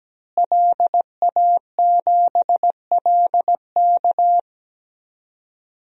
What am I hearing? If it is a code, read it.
Morse code, heard as a keyed tone, about 17 wpm, 700 Hz. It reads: LA7LK
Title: DXCC-100 Call Signs - 5 Characters - Repeated 2x - 17wpm